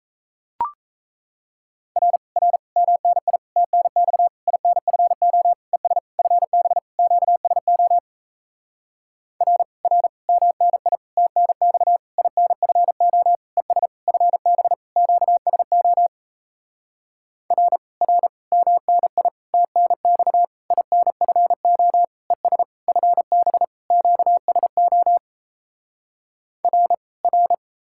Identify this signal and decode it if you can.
Morse code, heard as a keyed tone, about 38 wpm, 700 Hz. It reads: R R MNI TNX INFO ES FB QSO R R MNI TNX INFO ES FB QSO R R MNI TNX INFO ES FB QSO R R